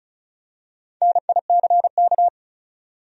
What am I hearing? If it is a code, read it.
Morse code, heard as a keyed tone, about 36 wpm, 700 Hz. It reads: NICK
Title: USA Common Names - 35wpm